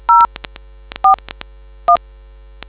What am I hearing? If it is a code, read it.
DTMF, heard as telephone touch tones, tones of about 110 ms. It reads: 041